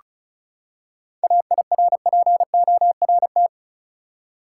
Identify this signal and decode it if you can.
Morse code, heard as a keyed tone, about 34 wpm, 700 Hz. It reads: AIRPORT
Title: Seven Letter Words - 35wpm